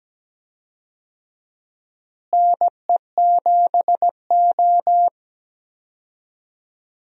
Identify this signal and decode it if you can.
Morse code, heard as a keyed tone, about 17 wpm, 700 Hz. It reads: NE7O
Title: USA Call Signs - 4 Characters - Repeated 2x - 17wpm